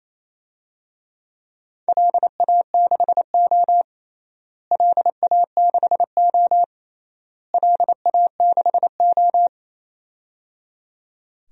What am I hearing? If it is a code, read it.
Morse code, heard as a keyed tone, about 28 wpm, 700 Hz. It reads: LA6O LA6O LA6O
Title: DXCC-100 Call Signs - 4 Characters - Repeated 3x - 28wpm